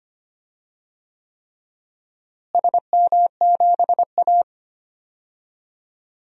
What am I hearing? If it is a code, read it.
Morse code, heard as a keyed tone, about 25 wpm, 700 Hz. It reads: SM7A
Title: DXCC-100 Call Signs - 4 Characters - Repeated 3x - 25wpm